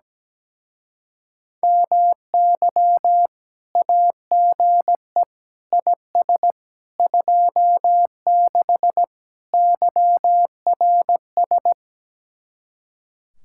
Morse code, heard as a keyed tone, about 17 wpm, 700 Hz. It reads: MY AGE IS 26 YRS
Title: Speed-Racing Common QSO Phrases - 17wpm